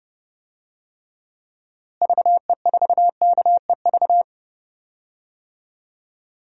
Morse code, heard as a keyed tone, about 30 wpm, 700 Hz. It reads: VE4KEV
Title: DXCC-100 Call Signs - 6 Characters - Repeated 3x - 30wpm